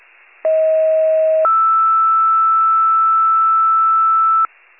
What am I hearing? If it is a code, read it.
Two-tone page, heard as a one-second tone, then a three-second tone, about 640 then 1320 Hz. It is heard from 0.4 to 4.5 s.
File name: pager / 46 RESCUE / wcalert